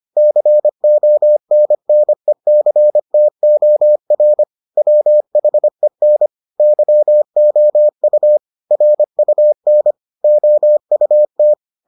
Morse code, heard as a keyed tone, about 25 wpm, 600 Hz. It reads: CONNECTOR WHEN YOU RUN OUT